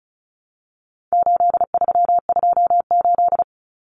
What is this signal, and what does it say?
Morse code, heard as a keyed tone, about 34 wpm, 700 Hz. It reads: 8328
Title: Numbers - 4-Digit 35wpm